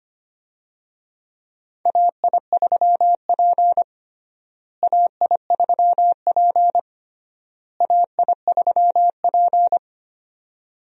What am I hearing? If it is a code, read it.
Morse code, heard as a keyed tone, about 25 wpm, 700 Hz. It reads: AI3P AI3P AI3P
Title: USA Call Signs - 4 Characters - Repeated 3x - 25wpm